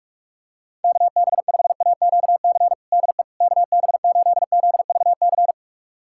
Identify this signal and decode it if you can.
Morse code, heard as a keyed tone, about 46 wpm, 700 Hz. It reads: KB5AQC DE KB8ZVC
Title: USA Callsign DE Callsign - 6 Characters - 45wpm